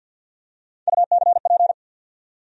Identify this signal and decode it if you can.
Morse code, heard as a keyed tone, about 51 wpm, 700 Hz. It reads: UCP
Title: USA Call Sign Postfixes - 3 Letters 50wpm